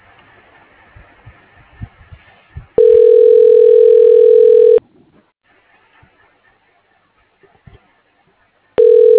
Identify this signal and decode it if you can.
Ringback tone (North America), 440 + 480 Hz.